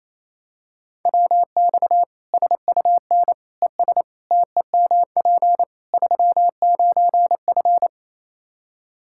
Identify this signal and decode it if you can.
Morse code, heard as a keyed tone, about 27 wpm, 700 Hz. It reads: WX SUN ES TEMP 39F
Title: Practice QSO - Exchange 2 - Info - 28wpm